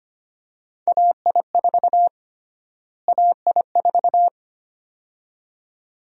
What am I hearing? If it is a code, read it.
Morse code, heard as a keyed tone, about 25 wpm, 700 Hz. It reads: AI4 AI4
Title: USA Call Sign Prefixes - Repeated 2x - 25wpm